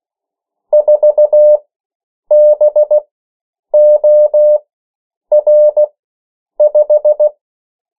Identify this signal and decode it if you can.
Morse code, heard as a keyed tone, about 16 wpm, spaced as for 8 wpm, 600 Hz. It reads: 4BOR5
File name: morse mp3 / 14 urok 13